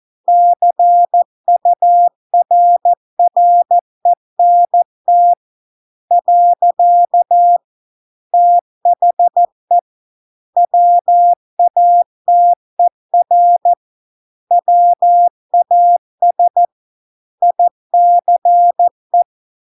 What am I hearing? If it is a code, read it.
Morse code, heard as a keyed tone, about 14 wpm, 700 Hz. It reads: CURRENT . THE WATER WAS ICE